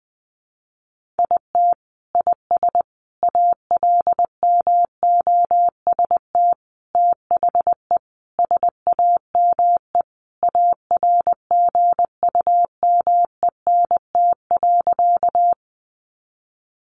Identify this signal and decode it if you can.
Morse code, heard as a keyed tone, about 20 wpm, 700 Hz. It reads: IT IS ALMOST THE SAME ARGUMENT.
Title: Sentences from Top 1400 Words 20wpm